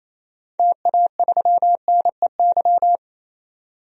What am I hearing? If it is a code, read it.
Morse code, heard as a keyed tone, about 28 wpm, 700 Hz. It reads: TA3NEY
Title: DXCC-100 Call Signs - 6 Characters - 28wpm